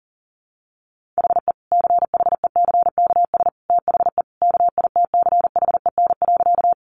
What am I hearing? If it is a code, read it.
Morse code, heard as a keyed tone, about 40 wpm, 700 Hz. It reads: HE CHECKS THE KITCHEN.
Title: Easy Phrases 40wpm